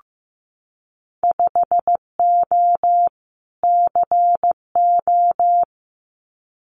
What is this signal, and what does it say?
Morse code, heard as a keyed tone, about 15 wpm, 700 Hz. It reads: 5O CO
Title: Winter Field Day Exchanges Encore 15wpm